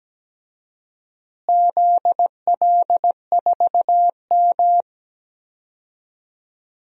Morse code, heard as a keyed tone, about 17 wpm, 700 Hz. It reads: ZL4M
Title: DXCC-100 Call Signs - 4 Characters - Repeated 2x - 17wpm